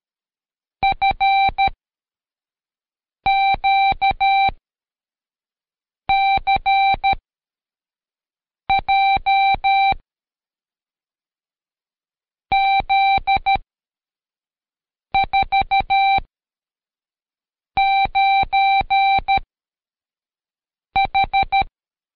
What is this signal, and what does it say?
Morse code, heard as a keyed tone, about 13 wpm, 780 Hz. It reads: F Q C J Z 4 9 H
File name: Code Quick / AudioDownld2 / Seg104-Day12-FiinalExam 1